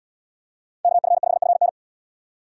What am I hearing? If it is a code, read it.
Morse code, heard as a keyed tone, about 69 wpm, 700 Hz. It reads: KF5FN